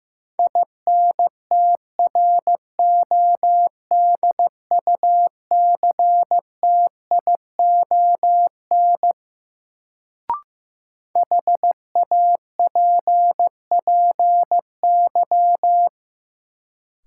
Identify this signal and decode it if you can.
Morse code, heard as a keyed tone, about 15 wpm, 700 Hz. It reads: INTRODUCTION HAPPY
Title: Sentences from Top 1000 Words - New Word Drill - 15wpm